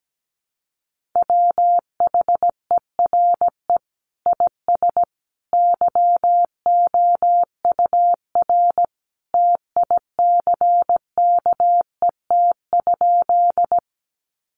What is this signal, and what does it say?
Morse code, heard as a keyed tone, about 17 wpm, 700 Hz. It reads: WHERE IS YOUR TICKET?
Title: Sentences from Top 1400 Words 17wpm